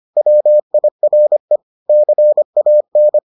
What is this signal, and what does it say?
Morse code, heard as a keyed tone, about 25 wpm, 600 Hz. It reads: WIRE CAN